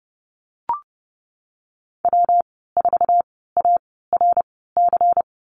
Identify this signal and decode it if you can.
Morse code, heard as a keyed tone, about 30 wpm, spaced as for 17 wpm, 700 Hz. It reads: W4ARC
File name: Lesson 37g - Letter x - Call Signs 30wpm